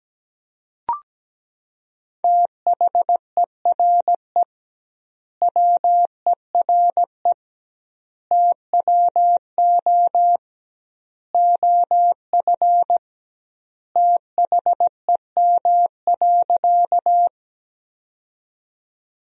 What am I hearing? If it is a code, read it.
Morse code, heard as a keyed tone, about 17 wpm, 700 Hz. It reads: THERE WERE TWO OF THEM.
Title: 2x Word-Spacing - Sentences from Top 100 Words - 17wpm